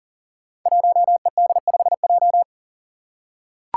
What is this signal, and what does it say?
Morse code, heard as a keyed tone, about 40 wpm, 700 Hz. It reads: 1ED5J
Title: Five Challenging Letter-Number Combinations 40wpm